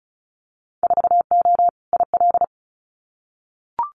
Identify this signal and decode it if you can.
Morse code, heard as a keyed tone, about 36 wpm, 700 Hz. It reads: 4O IL
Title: Winter Field Day Exchanges Encore 35wpm